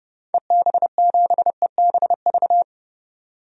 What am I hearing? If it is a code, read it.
Morse code, heard as a keyed tone, about 32 wpm, 690 Hz. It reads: EB7EBV
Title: DXCC-100 Call Signs - 6 Characters - 30wpm